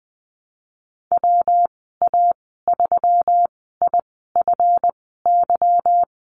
Morse code, heard as a keyed tone, about 20 wpm, spaced as for 14 wpm, 700 Hz. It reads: WA3IFY
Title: 245 Lesson 39g - Letter z - Call Signs 20wpm